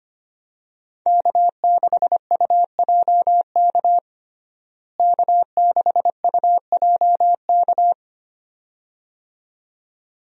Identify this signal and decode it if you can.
Morse code, heard as a keyed tone, about 25 wpm, 700 Hz. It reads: K6UJK K6UJK
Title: USA Call Signs - 5 Characters - Repeated 2x - 25wpm